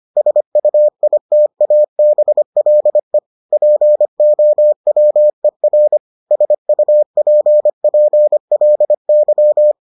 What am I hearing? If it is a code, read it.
Morse code, heard as a keyed tone, about 25 wpm, 600 Hz. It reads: SUITABLE POWER SUPPLY